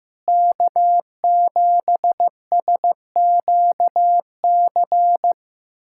Morse code, heard as a keyed tone, about 15 wpm, 700 Hz. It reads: K7SQC
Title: USA Callsign DE Callsign 15wpm